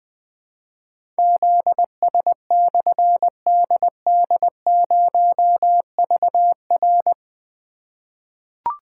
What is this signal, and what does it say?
Morse code, heard as a keyed tone, about 20 wpm, 700 Hz. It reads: ZS/DD0VR
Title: Difficult Call Signs 20wpm